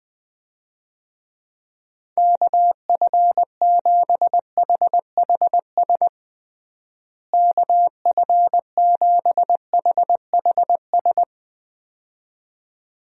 Morse code, heard as a keyed tone, about 20 wpm, 700 Hz. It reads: KF7HHS KF7HHS
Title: USA Call Signs - 6 Characters - Repeated 2x - 20wpm